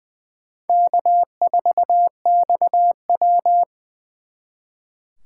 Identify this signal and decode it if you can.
Morse code, heard as a keyed tone, about 24 wpm, 700 Hz. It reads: K4XW
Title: Speed Racing - USA 1x2 Call Signs 20wpm